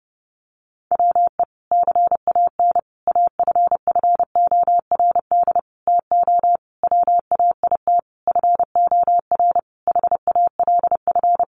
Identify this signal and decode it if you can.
Morse code, heard as a keyed tone, about 30 wpm, 700 Hz. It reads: WE CAN AFFORD TO WAIT FOR HALF
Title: Sentences from Top 1100 Words 30wpm